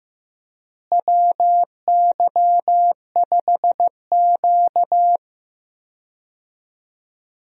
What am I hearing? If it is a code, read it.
Morse code, heard as a keyed tone, about 15 wpm, 700 Hz. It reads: WY5Q
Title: USA Call Signs - 4 Characters - Repeated 2x - 15wpm